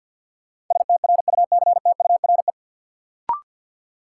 Morse code, heard as a keyed tone, about 47 wpm, 700 Hz. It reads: STRUCTURE
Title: Sentences from Top 1000 Words - Review Part 6 - New Word Drill - 50wpm